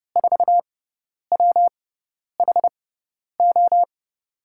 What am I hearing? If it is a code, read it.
Morse code, heard as a keyed tone, about 32 wpm, spaced as for 10 wpm, 700 Hz. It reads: 4WHO
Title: USA Call Signs - 5 Characters - 6x Character Spacing - 30wpm